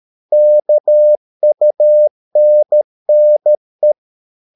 Morse code, heard as a keyed tone, about 13 wpm, 600 Hz. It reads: KUNNE
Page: Ferie 13 wpm